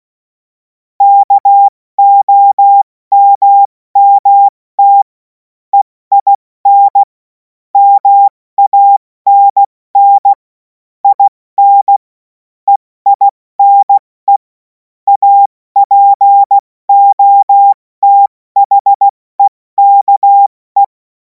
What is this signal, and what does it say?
Morse code, heard as a keyed tone, about 16 wpm, 800 Hz. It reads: KOMMT EIN MANN IN EINE APOTHEKE
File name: Morsekurs / Witze / Witz 17. März 2026 70 F80